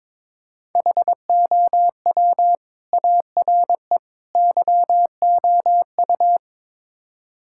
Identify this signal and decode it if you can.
Morse code, heard as a keyed tone, about 22 wpm, 700 Hz. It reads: HOW ARE YOU
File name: practice / 22wpm / Build A Phrase - 5-Word Phrases - 22wpm